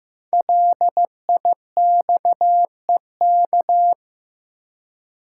Morse code, heard as a keyed tone, about 15 wpm, 700 Hz. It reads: LIXEK
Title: Code Groups - 5 Letter:Numbers - 15wpm